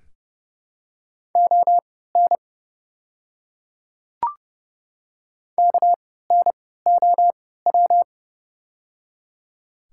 Morse code, heard as a keyed tone, about 31 wpm, spaced as for 17 wpm, 700 Hz. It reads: ON KNOW
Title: 3x Character Spacing - Top 100 Words - 30wpm